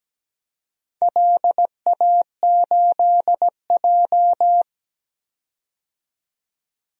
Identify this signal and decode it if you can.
Morse code, heard as a keyed tone, about 17 wpm, 700 Hz. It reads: LA8J
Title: DXCC-100 Call Signs - 4 Characters - 17wpm